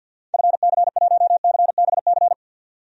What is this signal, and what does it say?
Morse code, heard as a keyed tone, about 49 wpm, 700 Hz. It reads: FC1CBC
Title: DXCC-100 Call Signs - 6 Characters - 50wpm